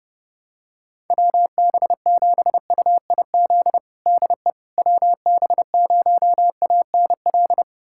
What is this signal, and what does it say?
Morse code, heard as a keyed tone, about 32 wpm, 700 Hz. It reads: WB7UIZ DE WB0ANL
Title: USA Callsign DE Callsign - 6 Characters - 30wpm